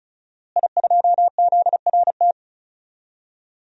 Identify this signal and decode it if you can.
Morse code, heard as a keyed tone, about 34 wpm, 700 Hz. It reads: I2ZRT
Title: DXCC-100 Call Signs - 5 Characters - Repeated 3x - 35wpm